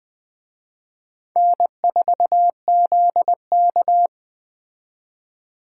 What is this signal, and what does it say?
Morse code, heard as a keyed tone, about 20 wpm, 700 Hz. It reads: N4ZK